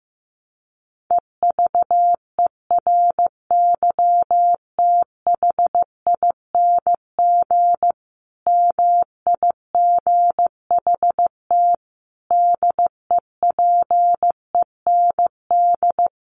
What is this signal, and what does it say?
Morse code, heard as a keyed tone, about 15 wpm, 700 Hz. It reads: EVERYTHING MIGHT DEPEND
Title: Sentences from Top 1300 Words 15wpm